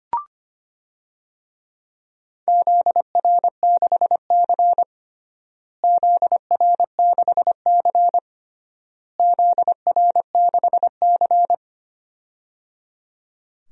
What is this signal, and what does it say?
Morse code, heard as a keyed tone, about 25 wpm, 700 Hz. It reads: ZR6C ZR6C ZR6C
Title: DXCC-100 Call Signs - 4 Characters - Repeated 3x - 25wpm